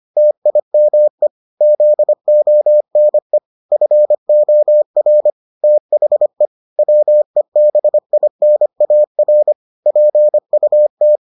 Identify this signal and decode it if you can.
Morse code, heard as a keyed tone, about 25 wpm, 600 Hz. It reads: TIME ZONE FOR THE WEBINAR PUT